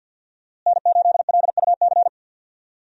Morse code, heard as a keyed tone, about 52 wpm, 700 Hz. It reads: N8LUC
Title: USA Call Signs - 5 Characters - Repeated 2x - 50wpm